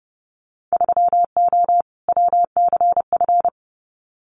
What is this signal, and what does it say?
Morse code, heard as a keyed tone, about 30 wpm, 700 Hz. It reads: 3O WCF
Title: Winter Field Day Exchanges 30wpm